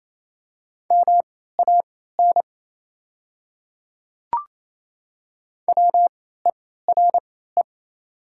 Morse code, heard as a keyed tone, about 28 wpm, spaced as for 16 wpm, 700 Hz. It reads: MAN WERE
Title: 3x Character Spacing - Top 500 Words - 28wpm